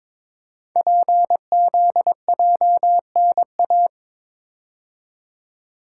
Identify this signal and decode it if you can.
Morse code, heard as a keyed tone, about 23 wpm, 700 Hz. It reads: PZJNA
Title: Code Groups - 5 Letters - 22wpm